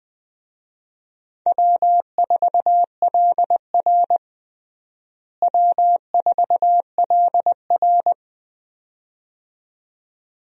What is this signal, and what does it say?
Morse code, heard as a keyed tone, about 21 wpm, 700 Hz. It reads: W4LR W4LR
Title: USA Call Signs - 4 Characters - Repeated 2x - 20wpm